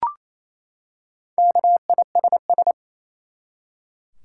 Morse code, heard as a keyed tone, about 28 wpm, 700 Hz. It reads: KISS
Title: Four Letter Words - 28wpm